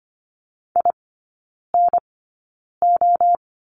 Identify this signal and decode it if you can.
Morse code, heard as a keyed tone, about 25 wpm, spaced as for 9 wpm, 700 Hz. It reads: INO
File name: course / 25:10wpm / 093 Lesson 16e - Letter c - Three Characters 25wpm